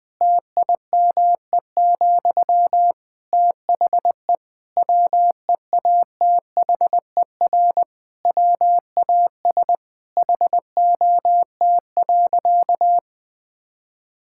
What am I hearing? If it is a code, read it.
Morse code, heard as a keyed tone, about 20 wpm, 700 Hz. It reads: TIME, THE WEATHER WAS HOT.
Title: Sentences from Top 1000 Words - Review Part 5 - 20wpm